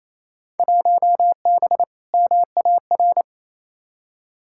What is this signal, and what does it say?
Morse code, heard as a keyed tone, about 28 wpm, 700 Hz. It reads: 1B MAR
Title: Field Day Exchanges Encore - 28wpm